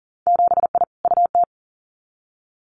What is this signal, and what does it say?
Morse code, heard as a keyed tone, about 42 wpm, 700 Hz. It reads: 7I UT
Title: Winter Field Day Exchanges Encore 40wpm